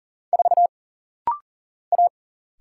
Morse code, heard as a keyed tone, about 40 wpm, 700 Hz. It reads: 4 A
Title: Single Letter-Number Alt - Rapid-Fire 40wpm